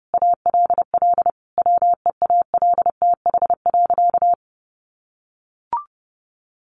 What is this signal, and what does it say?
Morse code, heard as a keyed tone, about 30 wpm, 700 Hz. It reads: ALL WEALTH.
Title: Sentences from Top 1200 Words 30wpm